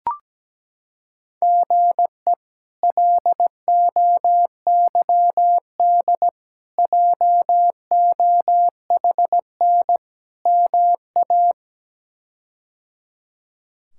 Morse code, heard as a keyed tone, about 17 wpm, 700 Hz. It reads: GE LOYD JOHN MA
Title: K1USN SST QSO Practice - 17wpm